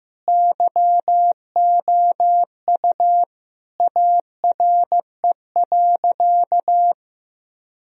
Morse code, heard as a keyed tone, about 16 wpm, 700 Hz. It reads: YOU ARE.
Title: Sentences from Top 1000 Words - Review Part 4 - 15wpm